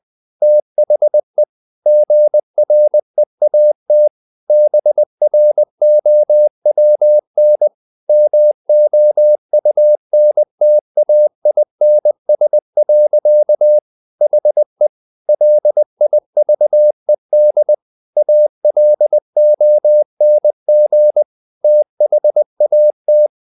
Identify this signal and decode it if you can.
Morse code, heard as a keyed tone, about 20 wpm, 600 Hz. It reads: THE GREAT BROWN MOUNTAINS. HE LIVED ALONG THAT
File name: the old man and the sea / the old man and the sea part 1 20wpm